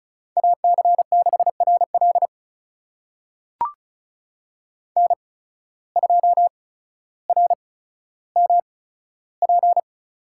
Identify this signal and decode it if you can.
Morse code, heard as a keyed tone, about 34 wpm, spaced as for 15 wpm, 700 Hz. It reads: AC6RL N2RMP
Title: USA Call Signs - 5 Characters - 8x Character Spacing - 35wpm